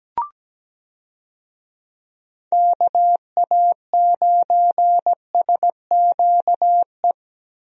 Morse code, heard as a keyed tone, about 17 wpm, 700 Hz. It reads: KA9SQE
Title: USA Call Signs - 6 Characters - Repeated 2x - 17wpm